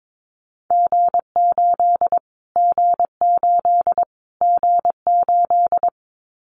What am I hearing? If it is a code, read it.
Morse code, heard as a keyed tone, about 22 wpm, 700 Hz. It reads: G8 G8 G8
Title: Callsign Prefixes for Top 25 DXCC - Repeated 3x - 22wpm